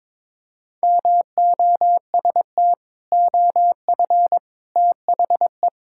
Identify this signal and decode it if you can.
Morse code, heard as a keyed tone, about 22 wpm, 700 Hz. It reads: MOST OF THE
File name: practice / 22wpm / Sentences with 3-Word Phrases - 22wpm